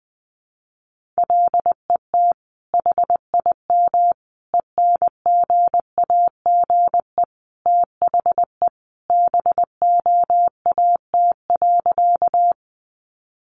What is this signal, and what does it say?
Morse code, heard as a keyed tone, about 20 wpm, 700 Hz. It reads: LET HIM ENGAGE THE BOAT.
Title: Sentences from Top 1500 Words 20wpm